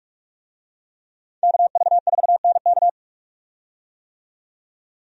Morse code, heard as a keyed tone, about 43 wpm, 700 Hz. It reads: KV4NK
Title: USA Call Signs - 5 Characters - 45wpm